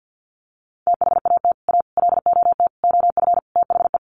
Morse code, heard as a keyed tone, about 48 wpm, 700 Hz. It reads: THAT A LOT OF THE
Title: Sets of 5 Words Encore2 50wpm